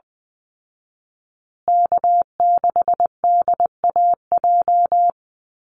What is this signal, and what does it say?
Morse code, heard as a keyed tone, about 20 wpm, 700 Hz. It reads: K6DAJ
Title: Call Signs Repeated 2x - 20wpm